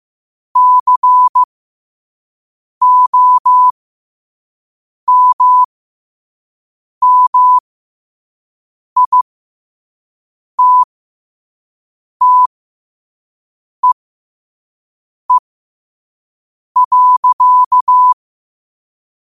Morse code, heard as a keyed tone, about 15 wpm, spaced as for 5 wpm, 1000 Hz. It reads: COMMITTEE.
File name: qotdfiles / 5wpm